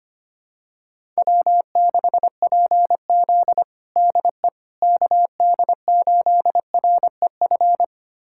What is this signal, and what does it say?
Morse code, heard as a keyed tone, about 25 wpm, 700 Hz. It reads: W6PZ DE KD8REF
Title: USA Callsign DE Callsign 25wpm